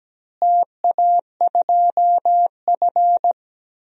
Morse code, heard as a keyed tone, about 17 wpm, 700 Hz. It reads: TA2F
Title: DXCC-100 Call Signs - 4 Characters - Repeated 2x - 17wpm